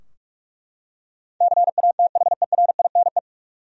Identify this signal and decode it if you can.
Morse code, heard as a keyed tone, about 41 wpm, 700 Hz. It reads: CATHERINE
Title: USA Common Names - 45wpm